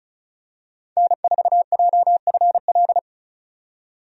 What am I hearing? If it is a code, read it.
Morse code, heard as a keyed tone, about 36 wpm, 700 Hz. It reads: N4JFL
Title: USA Call Signs - 5 Characters - 35wpm